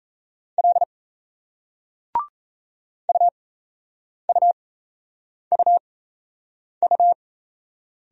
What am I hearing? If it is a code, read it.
Morse code, heard as a keyed tone, about 38 wpm, spaced as for 8 wpm, 700 Hz. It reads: L UUUU